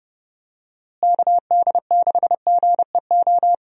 Morse code, heard as a keyed tone, about 30 wpm, 700 Hz. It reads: KD6GEO
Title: USA Call Signs - 6 Characters - Repeated 2x - 30wpm